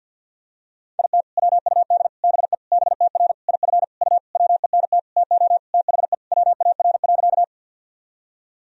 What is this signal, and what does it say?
Morse code, heard as a keyed tone, about 51 wpm, 700 Hz. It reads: IT WUD BE BTR IF U WENT TO THE WAR.
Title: Sentences from Top 400 Words and Abbreviations 50wpm